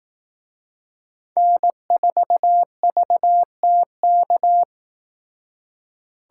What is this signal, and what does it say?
Morse code, heard as a keyed tone, about 18 wpm, 700 Hz. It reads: N4VTK
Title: Speed Racing - USA 1x3 Call Signs 15wpm